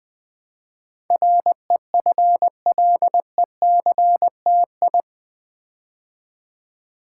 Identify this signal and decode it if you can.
Morse code, heard as a keyed tone, about 20 wpm, 700 Hz. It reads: REFLECTI
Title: Build A Word - 4 Letters At A Time - 10-Letter Words - 20wpm